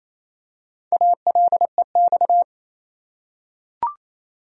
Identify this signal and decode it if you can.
Morse code, heard as a keyed tone, about 28 wpm, 700 Hz. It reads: ALEX
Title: USA Common Names - Max 5 Letters - 28wpm